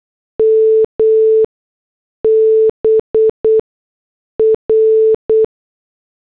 Morse code, heard as a keyed tone, about 8 wpm, spaced as for 6 wpm, 440 Hz. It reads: MBR